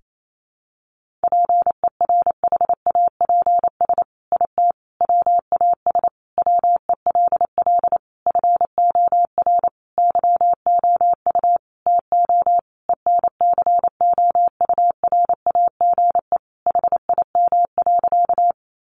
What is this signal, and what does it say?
Morse code, heard as a keyed tone, about 28 wpm, 700 Hz. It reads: PERHAPS IT WAS WELL FOR YOU TO ENCOURAGE HIM.
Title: Sentences from Top 1400 Words 28wpm